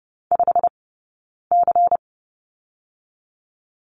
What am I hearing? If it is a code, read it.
Morse code, heard as a keyed tone, about 30 wpm, 700 Hz. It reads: 5 C
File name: 092 Lesson 16d - Letter c - Two Characters 30wpm